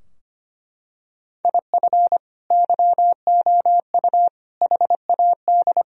Morse code, heard as a keyed tone, about 25 wpm, 700 Hz. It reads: IF YOU HAD
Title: Sentences from Top 1000 Words - Review Part 3 - 25wpm